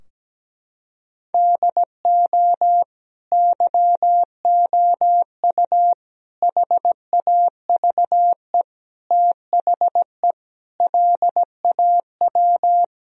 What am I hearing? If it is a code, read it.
Morse code, heard as a keyed tone, about 17 wpm, 700 Hz. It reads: DO YOU HAVE THE LAW
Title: Sentences from Top 1000 Words - Review Part 3 - 17wpm